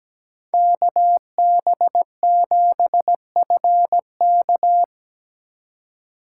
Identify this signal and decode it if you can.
Morse code, heard as a keyed tone, about 17 wpm, 700 Hz. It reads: KB7FK
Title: USA Call Signs - 5 Characters - 4x Character Spacing - 17wpm